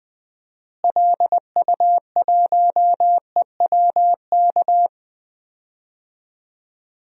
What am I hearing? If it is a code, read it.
Morse code, heard as a keyed tone, about 20 wpm, 700 Hz. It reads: LU1EWK
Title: DXCC-100 Call Signs - 6 Characters - Repeated 3x - 20wpm